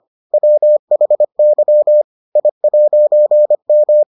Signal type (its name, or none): Morse code